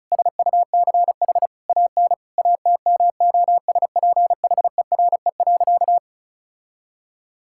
Morse code, heard as a keyed tone, about 36 wpm, 700 Hz. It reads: SUCH AN ATMOSPHERE.
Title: Sentences from Top 1600 Words 35wpm